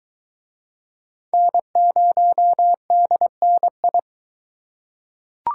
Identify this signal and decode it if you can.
Morse code, heard as a keyed tone, about 20 wpm, 700 Hz. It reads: N0DNI